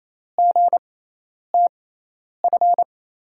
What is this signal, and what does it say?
Morse code, heard as a keyed tone, about 28 wpm, spaced as for 10 wpm, 700 Hz. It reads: GTF
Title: USA Call Signs - 5 Characters - 6x Character Spacing - 28wpm